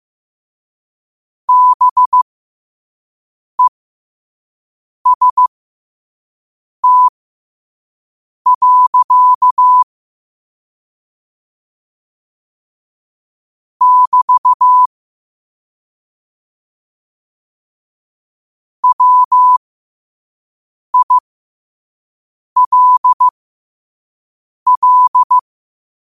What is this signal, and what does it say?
Morse code, heard as a keyed tone, about 15 wpm, spaced as for 5 wpm, 1000 Hz. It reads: BEST. <BT> WILL
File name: qotdfiles / 5wpm